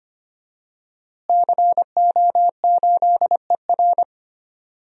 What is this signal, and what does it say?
Morse code, heard as a keyed tone, about 25 wpm, 700 Hz. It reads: CO8ER
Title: DXCC-100 Call Signs - 5 Characters - Repeated 2x - 25wpm